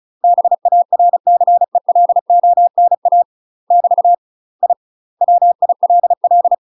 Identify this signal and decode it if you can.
Morse code, heard as a keyed tone, about 35 wpm, 700 Hz. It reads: BARCELONA <BT> I WILL
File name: UK MASTER QSOs TRAINING / 35WPM / UK MASTER QSOs 0005